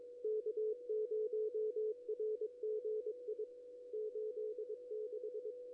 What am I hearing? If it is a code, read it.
Morse code, heard as a keyed tone, about 22 wpm, 440 Hz. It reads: K0RGI 86
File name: contests / ss-cw-k5tr-op-n5rz -2300